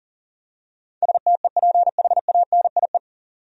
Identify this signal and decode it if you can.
Morse code, heard as a keyed tone, about 37 wpm, 700 Hz. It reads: STEPHANIE
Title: USA Common Names - 40wpm